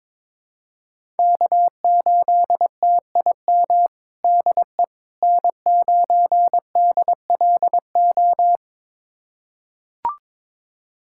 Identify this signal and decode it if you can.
Morse code, heard as a keyed tone, about 22 wpm, 700 Hz. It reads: K8TIM DE N9DLO
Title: USA Callsign DE Callsign - 5 Characters - 22wpm